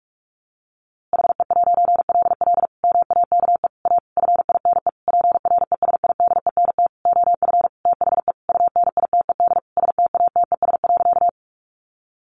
Morse code, heard as a keyed tone, about 44 wpm, 700 Hz. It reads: HE'LL MAKE A FINE PRESIDENT OF THE UNITED STATES.
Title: A Sound Of Thunder 45wpm